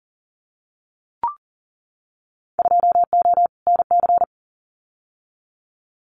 Morse code, heard as a keyed tone, about 41 wpm, 700 Hz. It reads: 2O NC
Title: Winter Field Day Exchanges 40wpm